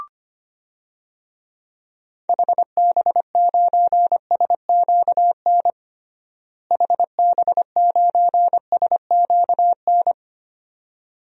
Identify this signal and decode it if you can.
Morse code, heard as a keyed tone, about 25 wpm, 700 Hz. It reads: HB9SQN HB9SQN
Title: DXCC-100 Call Signs - 6 Characters - Repeated 2x - 25wpm